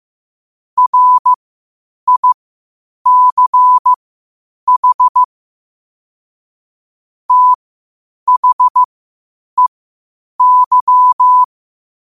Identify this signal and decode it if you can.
Morse code, heard as a keyed tone, about 15 wpm, spaced as for 8 wpm, 1000 Hz. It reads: RICH THEY
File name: qotdfiles / 8wpm